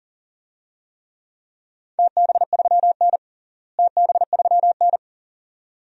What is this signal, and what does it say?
Morse code, heard as a keyed tone, about 40 wpm, 700 Hz. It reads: TB3N TB3N
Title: DXCC-100 Call Signs - 4 Characters - Repeated 2x - 40wpm